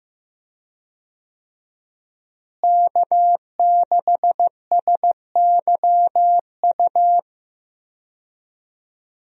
Morse code, heard as a keyed tone, about 16 wpm, 700 Hz. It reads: K6SYU
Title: USA Call Signs - 5 Characters - Repeated 2x - 15wpm